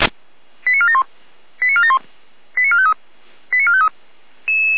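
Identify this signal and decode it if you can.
Five-tone selective call (ZVEI), heard as quick runs of five tones, tones of about 70 ms. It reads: 89471 89471 89453 89453
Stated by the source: ZVEI five-tone selective call received with an SDR.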